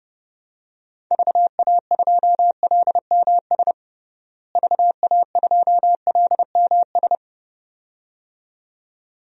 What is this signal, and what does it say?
Morse code, heard as a keyed tone, about 32 wpm, 700 Hz. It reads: VA2LMS VA2LMS
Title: DXCC-100 Call Signs - 6 Characters - Repeated 2x - 30wpm